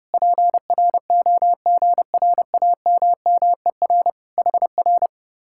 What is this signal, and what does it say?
Morse code, heard as a keyed tone, about 31 wpm, 700 Hz. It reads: PROGRAMMER HR